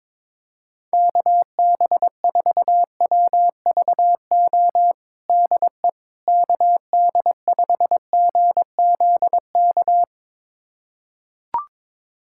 Morse code, heard as a keyed tone, about 22 wpm, 700 Hz. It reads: KB4WVO DE KD5GZK
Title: USA Callsign DE Callsign 22wpm